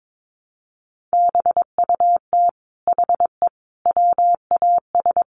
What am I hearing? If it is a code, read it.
Morse code, heard as a keyed tone, about 22 wpm, 700 Hz. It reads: BUT HE WAS
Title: Sentences from Top 700 Words 22wpm